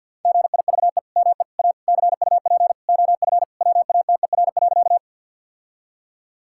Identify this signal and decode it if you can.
Morse code, heard as a keyed tone, about 50 wpm, 700 Hz. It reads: GIVE ME A CUP OF WATER.
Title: Sentences from Top 1600 Words 50wpm